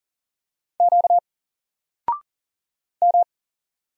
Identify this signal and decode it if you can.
Morse code, heard as a keyed tone, about 41 wpm, 700 Hz. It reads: Q M
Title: Single Letter-Number Alt 40wpm